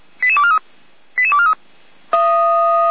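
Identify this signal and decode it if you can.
Five-tone selective call (ZVEI), heard as quick runs of five tones, tones of about 70 ms. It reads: 88243 88243
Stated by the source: ZVEI five-tone selective call received by SDR.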